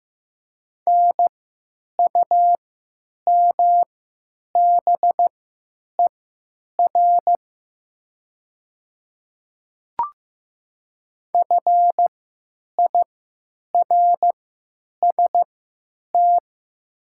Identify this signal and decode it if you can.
Morse code, heard as a keyed tone, about 15 wpm, spaced as for 9 wpm, 700 Hz. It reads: NUMBER FIRST
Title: 3x Character Spacing - Top 200 Words - 15wpm